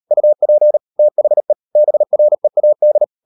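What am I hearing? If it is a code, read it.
Morse code, heard as a keyed tone, about 38 wpm, 600 Hz. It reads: UP THE BREAD
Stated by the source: Morse code practice at 35–39 wpm sending English text of Ken Follett's novel Never, from CW med Gnister.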